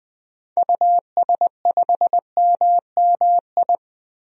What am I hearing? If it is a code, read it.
Morse code, heard as a keyed tone, about 20 wpm, 700 Hz. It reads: US5MMI
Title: DXCC-100 Call Signs - 6 Characters - 20wpm